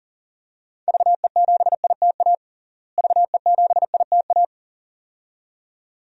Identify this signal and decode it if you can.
Morse code, heard as a keyed tone, about 37 wpm, 700 Hz. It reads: VE7ITA VE7ITA
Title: DXCC-100 Call Signs - 6 Characters - Repeated 2x - 40wpm